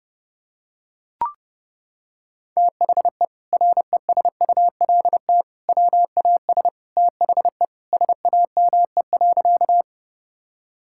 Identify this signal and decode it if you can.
Morse code, heard as a keyed tone, about 30 wpm, 700 Hz. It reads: THE RESULT WAS THE SAME.
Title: Sentences from Top 1000 Words - Review Part 5 - 30wpm